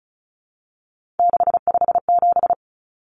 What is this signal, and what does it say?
Morse code, heard as a keyed tone, about 34 wpm, 700 Hz. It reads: 657
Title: Common Cut Numbers - 3-Digit 35wpm